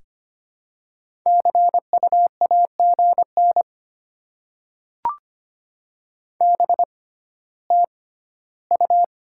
Morse code, heard as a keyed tone, about 26 wpm, spaced as for 14 wpm, 690 Hz. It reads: CU AGN BTU
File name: Top 20 QSO Elements - 6x Character Spacing - 25wpm